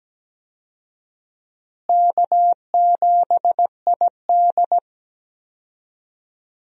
Morse code, heard as a keyed tone, about 17 wpm, 700 Hz. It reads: K7ID
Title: USA Call Signs - 4 Characters - Repeated 2x - 17wpm